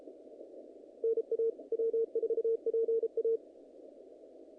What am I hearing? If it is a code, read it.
Morse code, heard as a keyed tone, about 33 wpm, 450 Hz. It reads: NA W4PA
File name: contests / naqp-cw-k5tr-op-k5pi- -2100